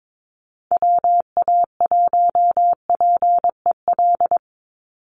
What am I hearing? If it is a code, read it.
Morse code, heard as a keyed tone, about 22 wpm, 700 Hz. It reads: WA1PEL
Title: USA Call Signs 22wpm